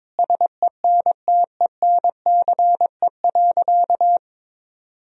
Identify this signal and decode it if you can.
Morse code, heard as a keyed tone, about 22 wpm, 700 Hz. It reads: SENTENCE.
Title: Sentences from Top 1000 Words - Review Part 1 - 22wpm